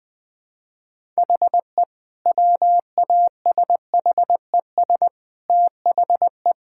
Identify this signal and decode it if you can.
Morse code, heard as a keyed tone, about 20 wpm, 700 Hz. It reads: HE WASHES THE
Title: Difficult Sentences with Hints - 20wpm